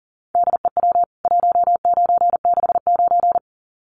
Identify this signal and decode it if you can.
Morse code, heard as a keyed tone, about 40 wpm, 700 Hz. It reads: DEW 1969
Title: CWT Exchange 40wpm